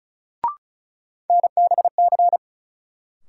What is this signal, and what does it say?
Morse code, heard as a keyed tone, about 36 wpm, 700 Hz. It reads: NBC
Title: Sets of 5 Words Encore - New Word Drill - 35wpm